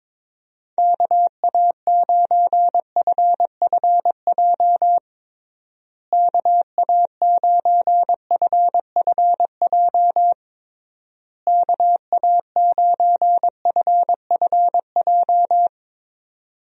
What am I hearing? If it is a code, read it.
Morse code, heard as a keyed tone, about 22 wpm, 700 Hz. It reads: KA9FFJ KA9FFJ KA9FFJ
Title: USA Call Signs - 6 Characters - Repeated 3x - 22wpm